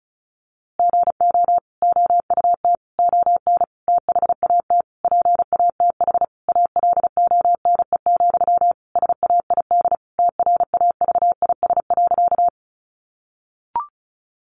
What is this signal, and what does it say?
Morse code, heard as a keyed tone, about 34 wpm, 700 Hz. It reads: GO OUT ON THAT PATH ALONE, SAID TRAVIS.
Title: A Sound Of Thunder 35wpm